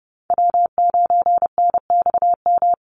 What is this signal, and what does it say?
Morse code, heard as a keyed tone, about 30 wpm, 700 Hz. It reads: W9NXM
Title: Call Signs Repeated 2x - 30wpm